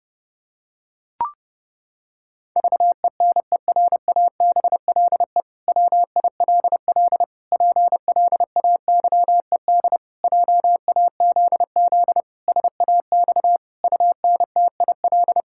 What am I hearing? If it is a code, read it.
Morse code, heard as a keyed tone, about 28 wpm, 700 Hz. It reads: VENERABLE WILL PLAYED JAZZ SAX UNTIL
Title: Pangram Challenge v2 - 30wpm